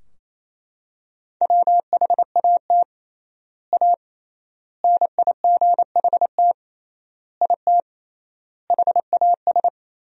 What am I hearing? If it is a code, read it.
Morse code, heard as a keyed tone, about 28 wpm, 700 Hz. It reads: WHAT A NIGHT IT HAS
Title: 3x Word-Spacing - Sentences from Top 300 Words - 28wpm